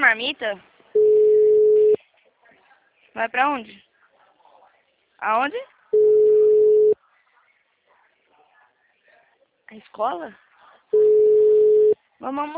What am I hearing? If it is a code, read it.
Ringback tone (Europe, 425 Hz), with speech.
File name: out- -2002- - - .14332